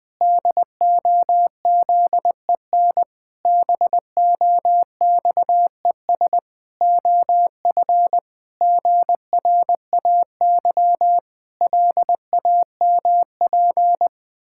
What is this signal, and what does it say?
Morse code, heard as a keyed tone, about 20 wpm, 700 Hz. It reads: DOZEN BOXES OF GRAY LAMP
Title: Pangram Challenge v2 - 20wpm